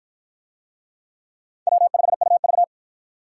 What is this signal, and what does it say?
Morse code, heard as a keyed tone, about 53 wpm, 700 Hz. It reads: W5UV